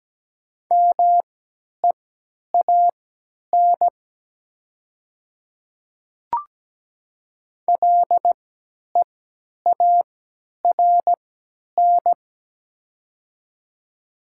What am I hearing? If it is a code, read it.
Morse code, heard as a keyed tone, about 17 wpm, spaced as for 10 wpm, 700 Hz. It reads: MEAN LEARN
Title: 3x Character Spacing - Top 300 Words - 17wpm